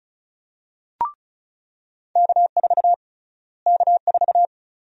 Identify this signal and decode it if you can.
Morse code, heard as a keyed tone, about 34 wpm, 700 Hz. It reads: K4 K4
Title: USA Call Sign Prefixes - Repeated 2x - 35wpm